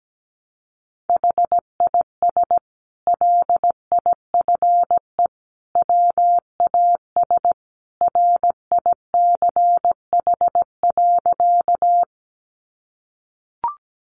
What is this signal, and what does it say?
Morse code, heard as a keyed tone, about 17 wpm, 700 Hz. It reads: HIS LIFE WAS RICH.
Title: Sentences from Top 1100 Words 17wpm